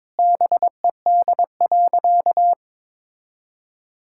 Morse code, heard as a keyed tone, about 21 wpm, 700 Hz. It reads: BED.
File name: practice / 22wpm / Sentences with 4-Word Phrases - 22wpm